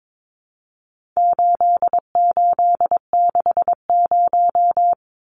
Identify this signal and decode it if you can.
Morse code, heard as a keyed tone, about 22 wpm, 700 Hz. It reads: 8860
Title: Numbers - 4-Digit 22wpm